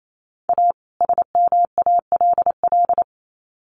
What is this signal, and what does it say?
Morse code, heard as a keyed tone, about 28 wpm, 700 Hz. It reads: A SMALL
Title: Sets of 2 Words 28wpm